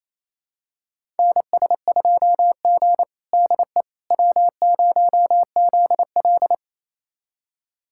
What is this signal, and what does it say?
Morse code, heard as a keyed tone, about 28 wpm, 700 Hz. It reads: NS2G DE W0ZL
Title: USA Callsign DE Callsign - 4 Characters - 28wpm